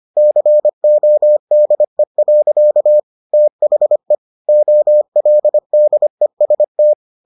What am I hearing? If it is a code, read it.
Morse code, heard as a keyed tone, about 25 wpm, 600 Hz. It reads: CODE. THE OLDEST